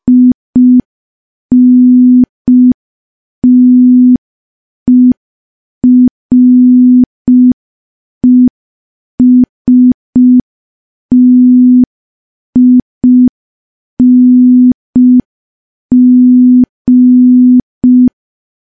Morse code, heard as a keyed tone, about 5 wpm, 250 Hz. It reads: INTERESTING
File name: morse / morse-training-05WPM